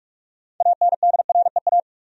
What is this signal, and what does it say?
Morse code, heard as a keyed tone, about 46 wpm, 700 Hz. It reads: ANDREA
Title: USA Common Names - 45wpm